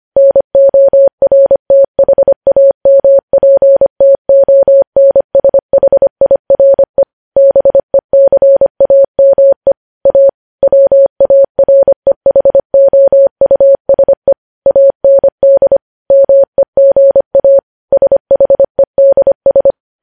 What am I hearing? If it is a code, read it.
Morse code, heard as a keyed tone, about 25 wpm, 550 Hz. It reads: NORTHAMPTONSHIRE BECAME A WAREHOUSE AND MEGA SHEDS
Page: News Headlines in Morse Code at 25 WPM